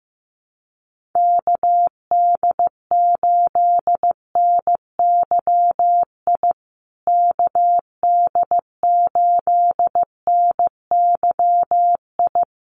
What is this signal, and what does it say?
Morse code, heard as a keyed tone, about 15 wpm, 700 Hz. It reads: KD8NYI KD8NYI
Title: USA Call Signs Repeated 3x 15wpm